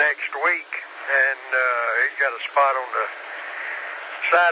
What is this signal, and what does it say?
Emergency Alert System (SAME) tones, heard as harsh buzzing bursts, with speech. It is heard from 0.4 to 4.1 s.